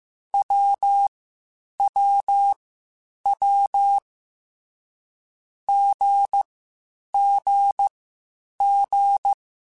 Morse code, heard as a keyed tone, about 15 wpm, spaced as for 8 wpm, 800 Hz. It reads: WWW GGG
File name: Short Runs / Short Runs-mp3s / Unballanced @13